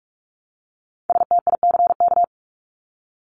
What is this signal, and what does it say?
Morse code, heard as a keyed tone, about 44 wpm, 700 Hz. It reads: STICK
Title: Top 1001-1250 Words 45wpm